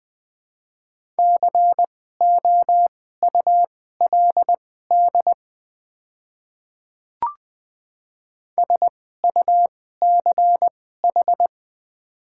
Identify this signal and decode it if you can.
Morse code, heard as a keyed tone, about 20 wpm, spaced as for 14 wpm, 700 Hz. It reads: COULD SUCH
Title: 2x Character Spacing - Top 200 Words - 20wpm